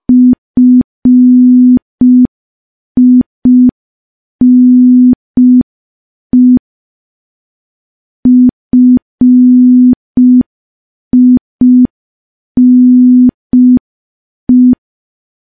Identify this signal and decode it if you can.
Morse code, heard as a keyed tone, about 5 wpm, 250 Hz. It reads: FINE FINE
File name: morse / morse-training-05WPM